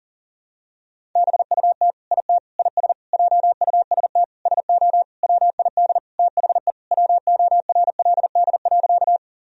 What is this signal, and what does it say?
Morse code, heard as a keyed tone, about 40 wpm, 700 Hz. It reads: BUT IT IS JUST SO WID THE WORLD.
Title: Sentences from Top 200 Words and Abbreviations 40wpm